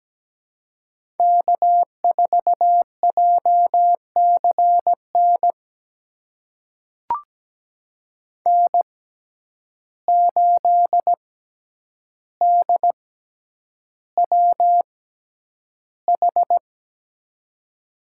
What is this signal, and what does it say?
Morse code, heard as a keyed tone, about 17 wpm, spaced as for 9 wpm, 700 Hz. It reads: K4JCN N8DWH
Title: USA Call Signs - 5 Characters - 6x Character Spacing - 17wpm